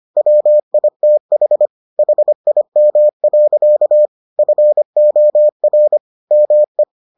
Morse code, heard as a keyed tone, about 25 wpm, 600 Hz. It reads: WITH HIM. FOR ME